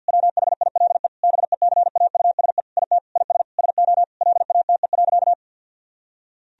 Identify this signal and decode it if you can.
Morse code, heard as a keyed tone, about 51 wpm, 700 Hz. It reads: WHILE BECAUSE IT IS SO LATE.
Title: Sentences from Top 1000 Words - Review Part 3 - 50wpm